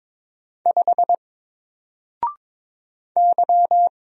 Morse code, heard as a keyed tone, about 22 wpm, 700 Hz. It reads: 5 Y
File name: Single-Letter-Number-22wpm